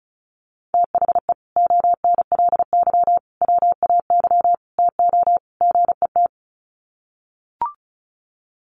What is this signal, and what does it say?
Morse code, heard as a keyed tone, about 34 wpm, 700 Hz. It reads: THE ONLY WAY TO GET
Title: Sets of 5 Words Encore2 35wpm